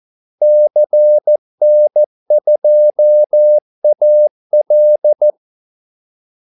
Morse code, heard as a keyed tone, about 14 wpm, 600 Hz. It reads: CN2AL
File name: Callsigns-14wpm